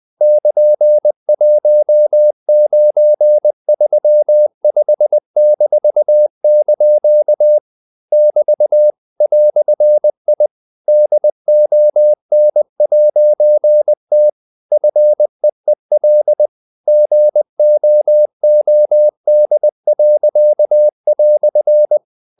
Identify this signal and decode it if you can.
Morse code, heard as a keyed tone, about 20 wpm, 600 Hz. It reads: <KN>1935-) <BT> "I DON'T FEEL GOOD."
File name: CW / Practice / EnglishQuotations0320